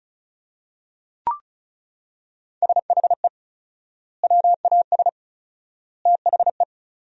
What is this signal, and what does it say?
Morse code, heard as a keyed tone, about 36 wpm, 700 Hz. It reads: SHE WAS THE
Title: 4x Word-Spacing - Sentences from Top 200 Words - 35wpm